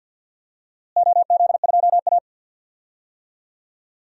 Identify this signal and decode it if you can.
Morse code, heard as a keyed tone, about 53 wpm, 700 Hz. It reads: OZ2A
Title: DXCC-100 Call Signs - 4 Characters - Repeated 3x - 50wpm